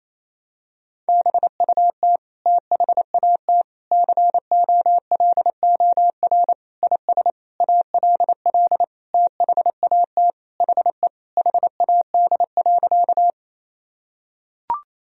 Morse code, heard as a keyed tone, about 28 wpm, 700 Hz. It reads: BUT THAT COLOR IS ALL THAT HE HAD.
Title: Sentences from Top 1000 Words - Review Part 2 - 28wpm